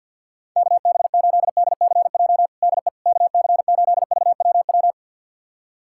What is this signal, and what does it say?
Morse code, heard as a keyed tone, about 47 wpm, 700 Hz. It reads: KB8DCJ DE KC8VWW
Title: USA Callsign DE Callsign - 6 Characters - 50wpm